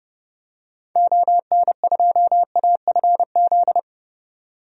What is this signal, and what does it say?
Morse code, heard as a keyed tone, about 30 wpm, 700 Hz. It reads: ON2AFZ
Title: DXCC-100 Call Signs - 6 Characters - 30wpm